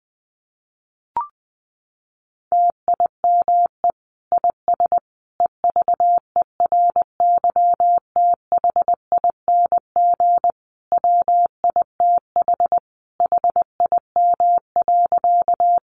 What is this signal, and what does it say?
Morse code, heard as a keyed tone, about 20 wpm, 700 Hz. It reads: TIME IS EVERYTHING WITH HIM.
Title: Sentences from Top 700 Words 20wpm